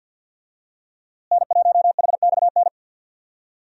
Morse code, heard as a keyed tone, about 46 wpm, 700 Hz. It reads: N1HXN
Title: USA Call Signs - 5 Characters - Repeated 3x - 50wpm